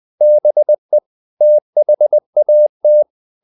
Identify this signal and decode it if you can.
Morse code, heard as a keyed tone, about 20 wpm, 600 Hz. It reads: BE THAT
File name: the old man and the sea / the old man and the sea part 1 20wpm